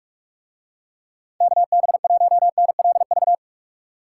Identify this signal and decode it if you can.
Morse code, heard as a keyed tone, about 41 wpm, 700 Hz. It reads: KB1NLV